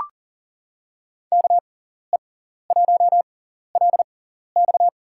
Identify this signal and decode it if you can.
Morse code, heard as a keyed tone, about 40 wpm, spaced as for 14 wpm, 700 Hz. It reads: KE1LX
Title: USA Call Signs - 5 Characters - 6x Character Spacing - 40wpm